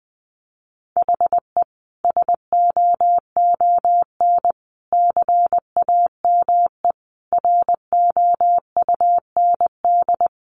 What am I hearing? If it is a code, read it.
Morse code, heard as a keyed tone, about 20 wpm, 700 Hz. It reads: HE SOON CAME ROUND
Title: Sentences from Top 600 Words 20wpm